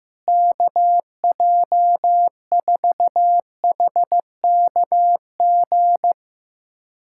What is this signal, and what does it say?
Morse code, heard as a keyed tone, about 15 wpm, 700 Hz. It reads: KJ4HKG
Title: USA Callsign DE Callsign - 6 Characters - 15wpm